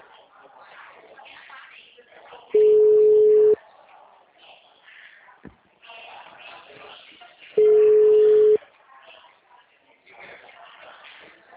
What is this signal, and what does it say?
Ringback tone (Europe, 425 Hz), with speech.